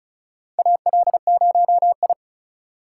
Morse code, heard as a keyed tone, about 36 wpm, 700 Hz. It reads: AL0I
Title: USA Call Signs - 4 Characters - 2x Character Spacing - 35wpm